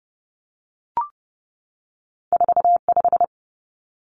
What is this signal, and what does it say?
Morse code, heard as a keyed tone, about 30 wpm, 700 Hz. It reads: 45
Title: Numbers - 2-Digit 30wpm